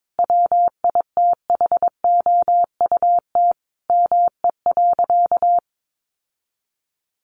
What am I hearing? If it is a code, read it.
Morse code, heard as a keyed tone, about 22 wpm, 700 Hz. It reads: WITHOUT ME.
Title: Sentences from Top 1200 Words 22wpm